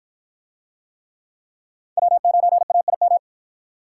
Morse code, heard as a keyed tone, about 47 wpm, 700 Hz. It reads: W9AIM